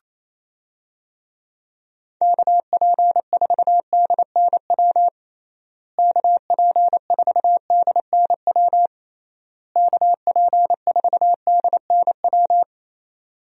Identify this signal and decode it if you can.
Morse code, heard as a keyed tone, about 28 wpm, 700 Hz. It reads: KP4DNW KP4DNW KP4DNW
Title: USA Call Signs - 6 Characters - Repeated 3x - 28wpm